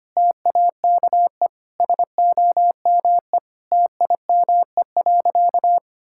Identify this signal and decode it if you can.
Morse code, heard as a keyed tone, about 25 wpm, 700 Hz. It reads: TAKE SOME TIME.
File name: practice / 25wpm / Sentences with 3-Word Phrases - 25wpm